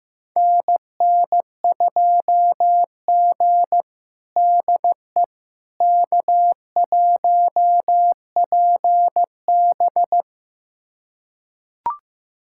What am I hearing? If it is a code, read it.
Morse code, heard as a keyed tone, about 15 wpm, 700 Hz. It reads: NN2G DE K1PB
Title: USA Callsign DE Callsign - 4 Characters - 15wpm